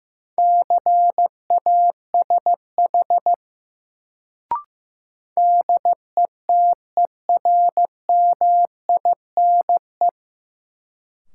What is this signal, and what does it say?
Morse code, heard as a keyed tone, about 15 wpm, 700 Hz. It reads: CASH DETERMINE
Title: Sentences from Top 700 Words - New Word Drill - 15wpm